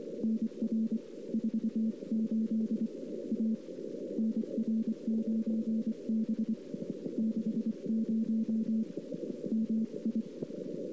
Morse code, heard as a keyed tone, about 24 wpm, 220 Hz. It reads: NR 48 A NR9B 60 MI
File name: CW / 2008-ss-cw-k5tr-op-n5rz -00:00